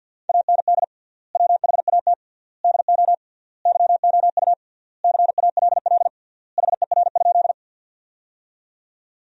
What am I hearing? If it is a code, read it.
Morse code, heard as a keyed tone, about 48 wpm, 700 Hz. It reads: AND WHAT DO YOU CALL HER?
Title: 3x Word-Spacing - Sentences from Top 100 Words - 50wpm